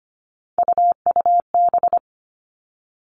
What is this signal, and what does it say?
Morse code, heard as a keyed tone, about 25 wpm, 700 Hz. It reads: UUB
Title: Three Challenging Letter-Number Combinations 25wpm